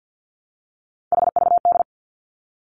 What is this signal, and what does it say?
Morse code, heard as a keyed tone, about 50 wpm, 700 Hz. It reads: HVD
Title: Three Challenging Letter-Number Combinations 50wpm